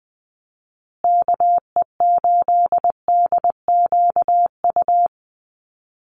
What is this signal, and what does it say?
Morse code, heard as a keyed tone, about 20 wpm, 700 Hz. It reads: KE8DQU
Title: USA Call Signs Repeated 3x 20wpm